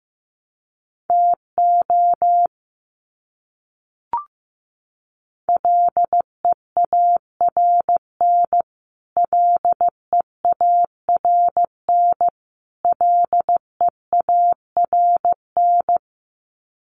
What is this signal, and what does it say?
Morse code, heard as a keyed tone, about 15 wpm, 700 Hz. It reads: TO LEARN LEARN LEARN
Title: Top 500 Words Repeated 15wpm